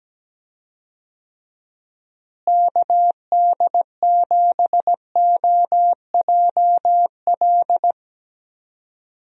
Morse code, heard as a keyed tone, about 17 wpm, 700 Hz. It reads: KD7OJL
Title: USA Call Signs - 6 Characters - Repeated 2x - 17wpm